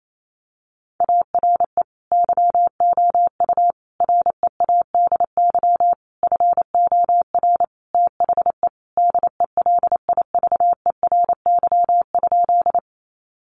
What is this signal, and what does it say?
Morse code, heard as a keyed tone, about 28 wpm, 700 Hz. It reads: ARE YOU READY FOR THE DELIVERY?
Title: Sentences from Top 1500 Words 28wpm